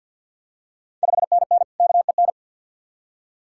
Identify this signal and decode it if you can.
Morse code, heard as a keyed tone, about 51 wpm, 690 Hz. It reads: 5NN KEN
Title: Ontario QSO Party Exchange 50wpm